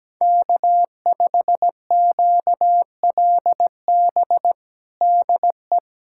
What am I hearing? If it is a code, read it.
Morse code, heard as a keyed tone, about 17 wpm, 700 Hz. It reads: K5QLB DE
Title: USA Callsign DE Callsign - 5 Characters - 17wpm